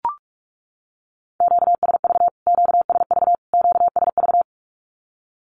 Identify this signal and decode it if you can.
Morse code, heard as a keyed tone, about 44 wpm, 700 Hz. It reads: QSV QSV QSV
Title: Q-Codes Repeated 3x 45wpm